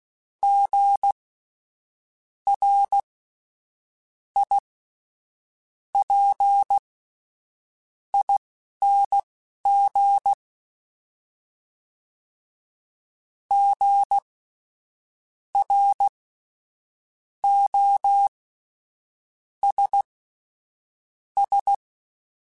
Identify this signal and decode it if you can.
Morse code, heard as a keyed tone, about 16 wpm, spaced as for 6 wpm, 800 Hz. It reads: GRIPING GROSS
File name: Spaced Alphas / Spaced Alphas mp3s / Gs sp @10